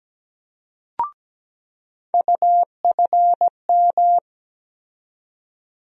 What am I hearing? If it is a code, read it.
Morse code, heard as a keyed tone, about 17 wpm, 700 Hz. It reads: UFM
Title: USA Call Sign Postfixes - 3 Letters 17wpm